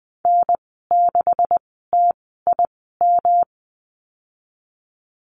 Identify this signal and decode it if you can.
Morse code, heard as a keyed tone, about 19 wpm, spaced as for 14 wpm, 700 Hz. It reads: N6TIM
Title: 186 Lesson 30g - Letter b - Call Signs 20wpm